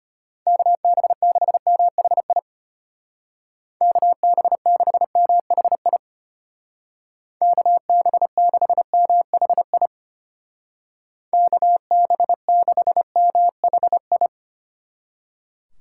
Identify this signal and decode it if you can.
Morse code, heard as a keyed tone, about 32 wpm, 700 Hz. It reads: KB6MHI KB6MHI KB6MHI KB6MHI